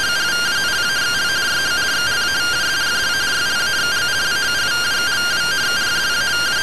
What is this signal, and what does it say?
Two-tone teleprinter signal, about 170 Hz shift.